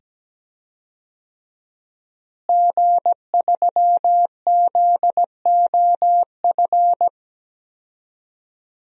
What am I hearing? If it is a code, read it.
Morse code, heard as a keyed tone, about 17 wpm, 700 Hz. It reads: G3ZOF
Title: DXCC-100 Call Signs - 5 Characters - Repeated 2x - 17wpm